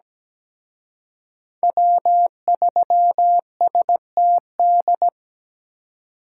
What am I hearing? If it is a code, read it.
Morse code, heard as a keyed tone, about 20 wpm, 700 Hz. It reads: W3STD
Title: Speed Racing - USA 1x3 Call Signs 17wpm